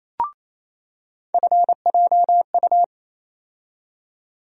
Morse code, heard as a keyed tone, about 27 wpm, 700 Hz. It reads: FJU
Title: USA Call Sign Postfixes - 3 Letters 28wpm